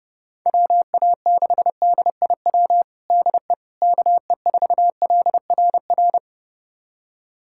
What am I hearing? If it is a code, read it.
Morse code, heard as a keyed tone, about 30 wpm, 700 Hz. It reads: WA6DIW DE KE4LRR
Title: USA Callsign DE Callsign 30wpm